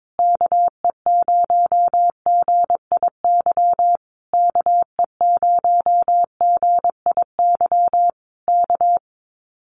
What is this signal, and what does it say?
Morse code, heard as a keyed tone, about 22 wpm, 700 Hz. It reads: KE0GIY KE0GIY K
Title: Common QSO Phrases Encore 22wpm